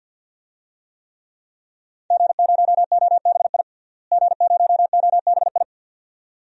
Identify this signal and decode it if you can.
Morse code, heard as a keyed tone, about 53 wpm, 700 Hz. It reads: G0OBI G0OBI
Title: DXCC-100 Call Signs - 5 Characters - Repeated 2x - 50wpm